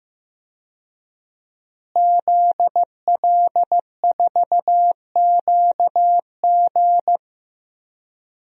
Morse code, heard as a keyed tone, about 15 wpm, 700 Hz. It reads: ZL4QG
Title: DXCC-100 Call Signs - 5 Characters - Repeated 2x - 15wpm